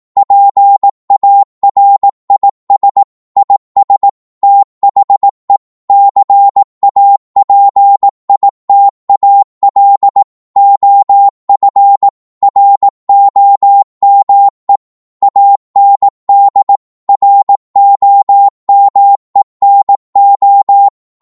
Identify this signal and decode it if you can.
Morse code, heard as a keyed tone, about 18 wpm, 800 Hz. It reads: PARIS IS THE CAPITAL OF ROME AND ROME NO